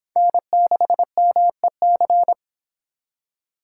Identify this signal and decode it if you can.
Morse code, heard as a keyed tone, about 24 wpm, 700 Hz. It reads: N6MEC